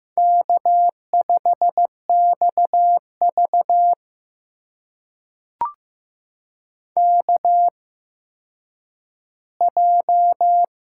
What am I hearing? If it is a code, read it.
Morse code, heard as a keyed tone, about 15 wpm, spaced as for 9 wpm, 700 Hz. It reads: K5XV KJ
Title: USA Call Signs - 8x Character Spacing - 15wpm